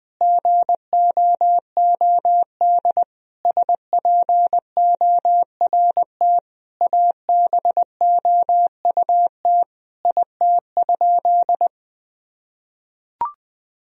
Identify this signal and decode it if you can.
Morse code, heard as a keyed tone, about 21 wpm, 700 Hz. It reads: GOOD SPORT ABOUT IT?
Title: Sentences from Top 1000 Words - Review Part 4 - 20wpm